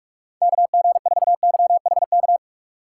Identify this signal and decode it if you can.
Morse code, heard as a keyed tone, about 46 wpm, 700 Hz. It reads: KG4YHK
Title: USA Call Signs - 6 Characters - 2x Character Spacing - 45wpm